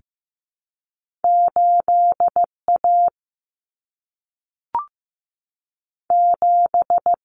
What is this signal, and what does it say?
Morse code, heard as a keyed tone, about 15 wpm, 700 Hz. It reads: 8A 7
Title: Common Cut Numbers - 2-Digit 15wpm